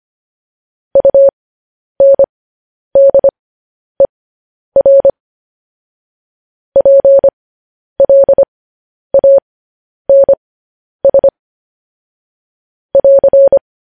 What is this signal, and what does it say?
Morse code, heard as a keyed tone, about 25 wpm, spaced as for 10 wpm, 550 Hz. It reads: UNDER PLANS <AR>
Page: news headlines in morse code at 10 wpm